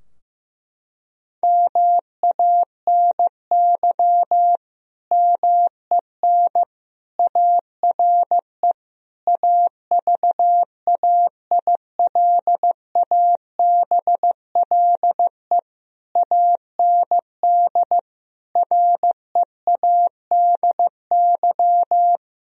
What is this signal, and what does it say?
Morse code, heard as a keyed tone, about 15 wpm, 700 Hz. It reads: MANY MEN ARE AVAILABLE AND READY
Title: Sentences from Top 1000 Words - Review Part 3 - 15wpm